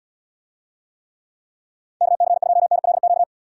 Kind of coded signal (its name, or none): Morse code